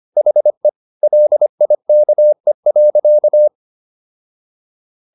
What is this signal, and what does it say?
Morse code, heard as a keyed tone, about 25 wpm, 600 Hz. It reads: HE LIKE.